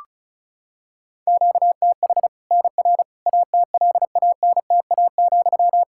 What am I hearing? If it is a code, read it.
Morse code, heard as a keyed tone, about 34 wpm, 700 Hz. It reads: QTH NR ATLANTA,
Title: K1USN SST QSO Practice 35wpm